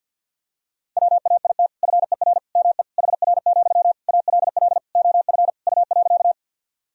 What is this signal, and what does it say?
Morse code, heard as a keyed tone, about 49 wpm, 700 Hz. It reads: WAIT FER ME HR, ALL OF U.
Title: Sentences from Top 500 Words and Abbreviations 50wpm